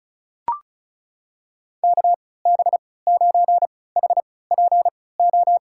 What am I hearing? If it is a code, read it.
Morse code, heard as a keyed tone, about 35 wpm, spaced as for 20 wpm, 690 Hz. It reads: KB9HPO
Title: USA Call Signs - 6 Characters - 3x Character Spacing - 35wpm